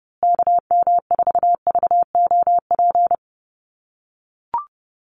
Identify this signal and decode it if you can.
Morse code, heard as a keyed tone, about 30 wpm, 700 Hz. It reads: KM4VOP
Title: USA Call Signs Repeated 2x 30wpm